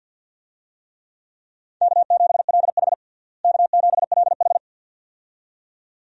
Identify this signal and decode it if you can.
Morse code, heard as a keyed tone, about 53 wpm, 700 Hz. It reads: K7LH K7LH
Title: USA Call Signs - 4 Characters - Repeated 2x - 50wpm